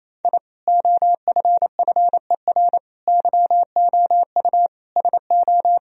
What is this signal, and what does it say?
Morse code, heard as a keyed tone, about 28 wpm, 700 Hz. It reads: I OFFER YOU SO
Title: Sentences from Top 1000 Words - Review Part 4 - 28wpm